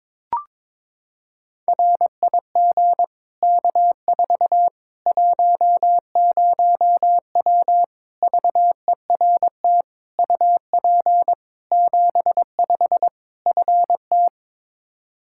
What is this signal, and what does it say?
Morse code, heard as a keyed tone, about 22 wpm, 700 Hz. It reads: RIG K4 10W VERT UP 75 FT
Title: Practice QSO - Exchange 2 - Info - 22wpm